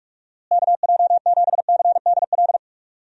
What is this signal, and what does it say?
Morse code, heard as a keyed tone, about 44 wpm, 700 Hz. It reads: KJ7CDL
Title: USA Call Signs - 6 Characters - Repeated 3x - 45wpm